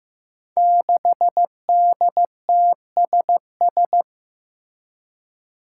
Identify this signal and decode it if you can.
Morse code, heard as a keyed tone, about 15 wpm, 700 Hz. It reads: 6DTSS
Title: Five Challenging Letter-Number Combinations 15wpm